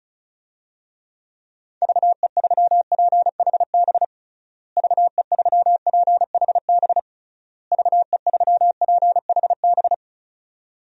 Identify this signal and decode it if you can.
Morse code, heard as a keyed tone, about 36 wpm, 700 Hz. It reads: VE3PHB VE3PHB VE3PHB
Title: DXCC-100 Call Signs - 6 Characters - Repeated 3x - 35wpm